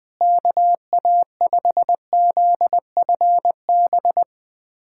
Morse code, heard as a keyed tone, about 21 wpm, 700 Hz. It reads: KA5ZFB
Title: USA Callsign DE Callsign - 6 Characters - 20wpm